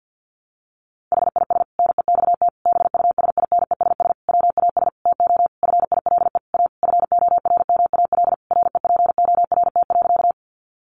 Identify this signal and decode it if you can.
Morse code, heard as a keyed tone, about 50 wpm, 700 Hz. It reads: HIS NEXT BUSINESS WAS TO FILE A FORMAL REPORT.
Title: Sentences from Top 1300 Words 50wpm